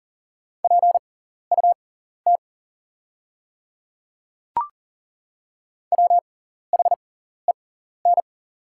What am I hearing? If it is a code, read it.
Morse code, heard as a keyed tone, about 40 wpm, spaced as for 14 wpm, 700 Hz. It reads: PUT WHEN
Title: 6x Character Spacing - Top 200 Words - 40wpm